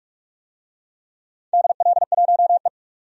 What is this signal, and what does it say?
Morse code, heard as a keyed tone, about 44 wpm, 700 Hz. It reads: DL1E
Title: DXCC-100 Call Signs - 4 Characters - 45wpm